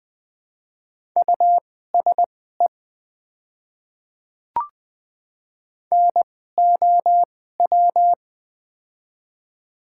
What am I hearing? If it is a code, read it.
Morse code, heard as a keyed tone, about 20 wpm, spaced as for 15 wpm, 700 Hz. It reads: USE NOW
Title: 2x Character Spacing - Top 100 Words - 20wpm